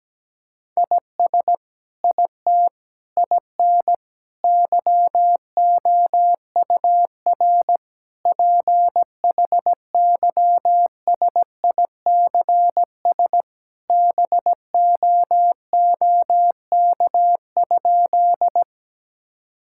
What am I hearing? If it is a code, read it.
Morse code, heard as a keyed tone, about 17 wpm, 700 Hz. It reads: IS IT IN YOUR PHYSICS BOOK?
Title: Sentences from Top 1000 Words - Review Part 5 - 17wpm